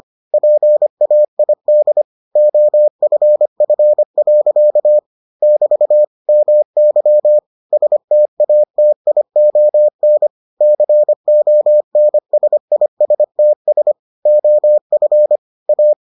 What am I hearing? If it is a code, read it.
Morse code, heard as a keyed tone, about 25 wpm, 600 Hz. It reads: PAID OFF. <BT> MY STATION CONSISTS OF A